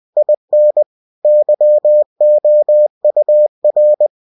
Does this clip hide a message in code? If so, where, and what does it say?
Yes: Morse code from 0.2 to 4.1 s, heard as a keyed tone, about 20 wpm, 600 Hz. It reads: IN YOUR